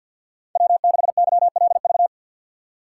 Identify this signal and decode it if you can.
Morse code, heard as a keyed tone, about 50 wpm, 700 Hz. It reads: W6YLV
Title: USA Call Signs - 5 Characters - 2x Character Spacing - 50wpm